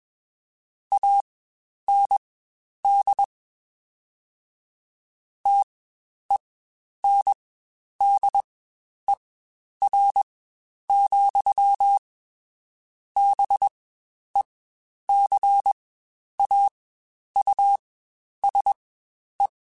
Morse code, heard as a keyed tone, about 21 wpm, spaced as for 10 wpm, 800 Hz. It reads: AND TENDER, BECAUSE
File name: Spaced / Chuckles Spaced-mp3s / SoftWords sp @20